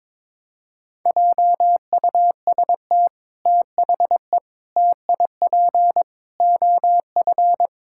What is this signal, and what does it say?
Morse code, heard as a keyed tone, about 21 wpm, 700 Hz. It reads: JUST THE TIP OF
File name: practice / 22wpm / Sets of 6 Words - 22wpm